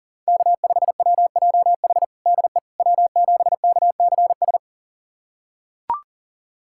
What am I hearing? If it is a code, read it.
Morse code, heard as a keyed tone, about 40 wpm, 700 Hz. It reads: K5WJH DE W7KCS
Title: USA Callsign DE Callsign - 5 Characters - 40wpm